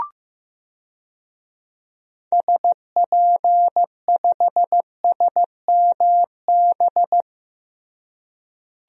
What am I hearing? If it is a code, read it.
Morse code, heard as a keyed tone, about 15 wpm, 700 Hz. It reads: SP5SMB
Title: DXCC-100 Call Signs - 6 Characters - Repeated 2x - 15wpm